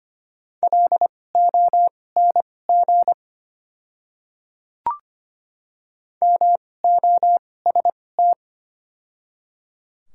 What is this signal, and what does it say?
Morse code, heard as a keyed tone, about 25 wpm, spaced as for 18 wpm, 690 Hz. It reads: LONG MOST
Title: 2x Character Spacing - Top 100 Words - 25wpm